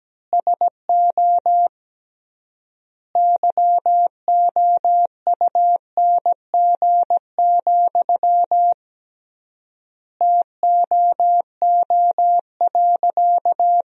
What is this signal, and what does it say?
Morse code, heard as a keyed tone, about 17 wpm, 700 Hz. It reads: SO YOUNG, TOO.
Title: 3x Word-Spacing - Sentences from Top 400 Words - 17wpm